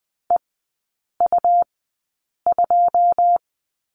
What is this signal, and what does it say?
Morse code, heard as a keyed tone, about 20 wpm, spaced as for 8 wpm, 700 Hz. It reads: EU2
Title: 100 Lesson 17e - Letter u - Three Characters 20wpm